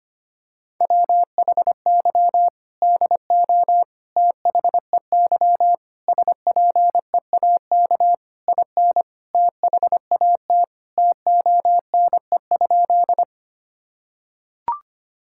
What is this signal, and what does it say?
Morse code, heard as a keyed tone, about 25 wpm, 700 Hz. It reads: WHY DO THEY SPEAK IN THAT TONE?
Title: Sentences from Top 1600 Words 25wpm